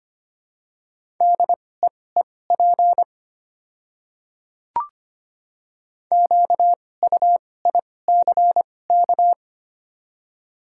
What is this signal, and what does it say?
Morse code, heard as a keyed tone, about 25 wpm, spaced as for 18 wpm, 700 Hz. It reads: DEEP QUICK
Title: 2x Character Spacing - Top 500 Words - 25wpm